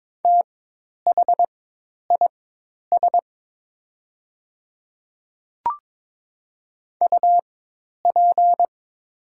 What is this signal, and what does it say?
Morse code, heard as a keyed tone, about 22 wpm, spaced as for 10 wpm, 700 Hz. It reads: THIS UP
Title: 4x Character Spacing - Top 300 Words - 22wpm